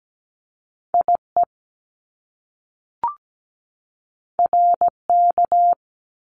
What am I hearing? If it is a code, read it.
Morse code, heard as a keyed tone, about 17 wpm, 700 Hz. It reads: IE RK
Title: Top Two Letter Combinations in Words 17wpm